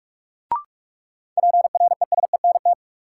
Morse code, heard as a keyed tone, about 46 wpm, 700 Hz. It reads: PRESENT
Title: Sentences from Top 800 Words - New Word Drill - 45wpm